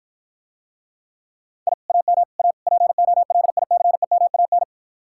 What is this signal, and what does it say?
Morse code, heard as a keyed tone, about 52 wpm, 700 Hz. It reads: I AM A POLICEMAN